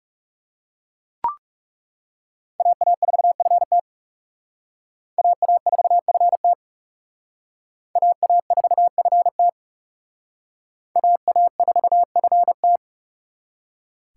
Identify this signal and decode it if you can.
Morse code, heard as a keyed tone, about 37 wpm, 700 Hz. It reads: AA4FT AA4FT AA4FT AA4FT
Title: Speed Racing - USA 2x2 Call Signs 30wpm